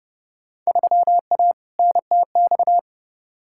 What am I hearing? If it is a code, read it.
Morse code, heard as a keyed tone, about 32 wpm, 700 Hz. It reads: 3A NTX
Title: Field Day Exchanges - 30wpm